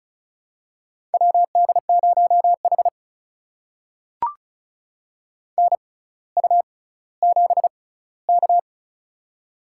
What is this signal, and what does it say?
Morse code, heard as a keyed tone, about 34 wpm, spaced as for 18 wpm, 700 Hz. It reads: WD0H NU7K
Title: USA Call Signs - 4 Characters - 6x Character Spacing - 35wpm